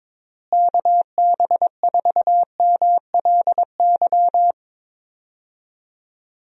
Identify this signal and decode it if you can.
Morse code, heard as a keyed tone, about 22 wpm, 700 Hz. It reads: KB4MLY
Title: USA Call Signs - 6 Characters - Repeated 2x - 22wpm